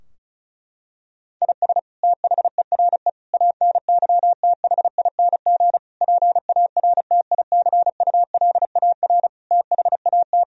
Morse code, heard as a keyed tone, about 36 wpm, 700 Hz. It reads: IS THERE ANYTHING PARTICULAR THAT
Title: Sentences from Top 1000 Words - Review Part 4 - 35wpm